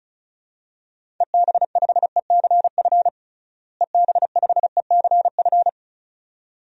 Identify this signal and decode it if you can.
Morse code, heard as a keyed tone, about 36 wpm, 700 Hz. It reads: EB5ECF EB5ECF
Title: DXCC-100 Call Signs - 6 Characters - Repeated 2x - 35wpm